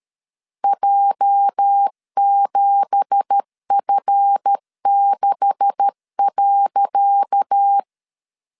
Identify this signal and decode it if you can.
Morse code, heard as a keyed tone, about 13 wpm, 800 Hz. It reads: J7F6.
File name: LEZIONI / 10 SIGNS / 10D COMA 12wpm 2544